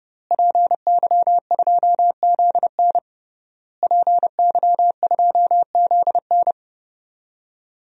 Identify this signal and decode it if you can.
Morse code, heard as a keyed tone, about 30 wpm, 700 Hz. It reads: PY2ZN PY2ZN
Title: DXCC-100 Call Signs - 5 Characters - Repeated 3x - 30wpm